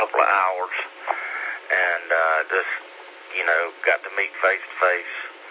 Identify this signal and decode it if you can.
Emergency Alert System (SAME) tones, heard as harsh buzzing bursts, with speech. It is heard from 1.2 to 4.3 s.